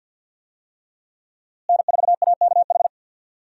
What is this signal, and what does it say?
Morse code, heard as a keyed tone, about 52 wpm, 700 Hz. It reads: N4AKH
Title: USA Call Signs - 5 Characters - Repeated 3x - 50wpm